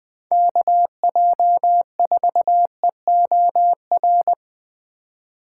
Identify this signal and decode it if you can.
Morse code, heard as a keyed tone, about 20 wpm, 700 Hz. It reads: KJ4EOR
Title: USA Call Signs - 6 Characters - 2x Character Spacing - 20wpm